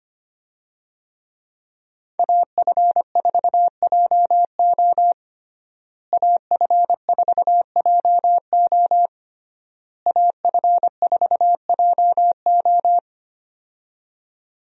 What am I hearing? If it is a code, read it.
Morse code, heard as a keyed tone, about 25 wpm, 700 Hz. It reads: AF4JO AF4JO AF4JO
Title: USA Call Signs - 5 Characters - Repeated 3x - 25wpm